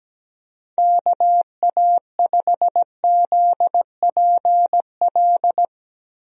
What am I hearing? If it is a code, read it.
Morse code, heard as a keyed tone, about 17 wpm, 700 Hz. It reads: KA5ZPL
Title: USA Call Signs - 6 Characters - Repeated 2x - 17wpm